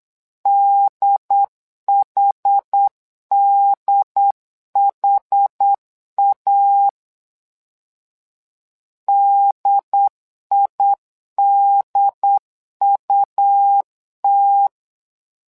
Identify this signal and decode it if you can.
Morse code, heard as a keyed tone, about 8 wpm, 800 Hz. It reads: DHDHA DIDUT
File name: LEZIONI / 2 UHDN / 2 UHD 8wpm 2258